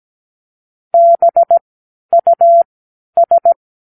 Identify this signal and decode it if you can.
Morse code, heard as a keyed tone, about 17 wpm, spaced as for 10 wpm, 680 Hz. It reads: BUS
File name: k7qo / Practicewords